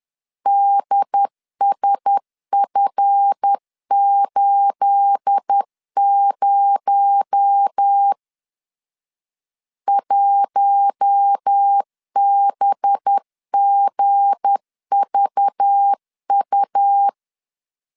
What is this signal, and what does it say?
Morse code, heard as a keyed tone, about 10 wpm, 800 Hz. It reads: DSF80 1BGVU
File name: CW / DET / 1 9 1500C S A 10wpm 4511